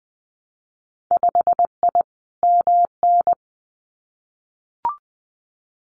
Morse code, heard as a keyed tone, about 19 wpm, 700 Hz. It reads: 5I MN
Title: Winter Field Day Exchanges Encore 20wpm